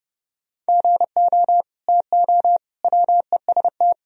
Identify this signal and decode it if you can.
Morse code, heard as a keyed tone, about 30 wpm, 700 Hz. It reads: GO TO WEST
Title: Sentences from Top 1000 Words - Review Part 2 - 30wpm